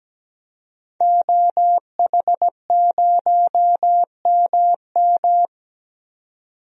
Morse code, heard as a keyed tone, about 17 wpm, 700 Hz. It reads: OH0MM
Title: DXCC-100 Call Signs - 5 Characters - 17wpm